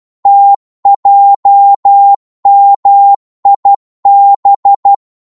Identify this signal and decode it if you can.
Morse code, heard as a keyed tone, about 12 wpm, 800 Hz. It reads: TJMIB
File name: Morse / ltr-12wpm